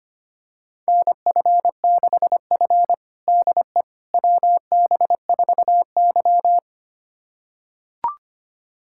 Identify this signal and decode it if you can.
Morse code, heard as a keyed tone, about 25 wpm, 700 Hz. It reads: NF6F DE WB4Y
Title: USA Callsign DE Callsign - 4 Characters - 25wpm